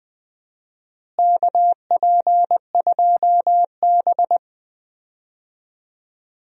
Morse code, heard as a keyed tone, about 20 wpm, 700 Hz. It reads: KP2B
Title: DXCC-100 Call Signs - 4 Characters - Repeated 3x - 20wpm